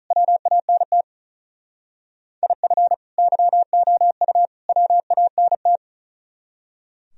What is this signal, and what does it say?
Morse code, heard as a keyed tone, about 42 wpm, 700 Hz. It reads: WANT IF YOU WANT
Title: Speed-Racing - Sets of 3 Words 35wpm